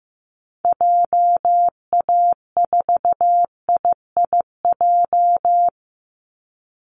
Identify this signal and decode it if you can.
Morse code, heard as a keyed tone, about 15 wpm, 700 Hz. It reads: JA4IIJ
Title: Call Signs Repeated 3x - 15wpm